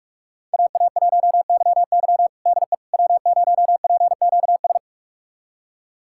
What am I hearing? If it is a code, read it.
Morse code, heard as a keyed tone, about 41 wpm, 700 Hz. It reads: AA1YY DE W0PQS
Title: USA Callsign DE Callsign - 5 Characters - 45wpm